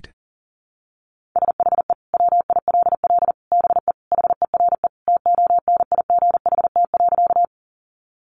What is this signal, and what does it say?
Morse code, heard as a keyed tone, about 40 wpm, 700 Hz. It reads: SHE WILL BE HERE TONIGHT.
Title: Sentences from Top 600 Words 40wpm